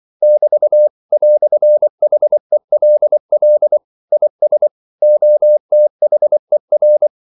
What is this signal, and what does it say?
Morse code, heard as a keyed tone, about 24 wpm, 600 Hz. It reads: <BT> "HELL IS OTHER
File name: CW / Practice / EnglishQuotations0324